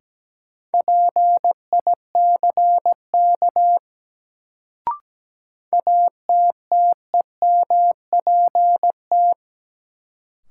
Morse code, heard as a keyed tone, about 17 wpm, 700 Hz. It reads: PICK ATTEMPT
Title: Sentences from Top 1000 Words - New Word Drill - 17wpm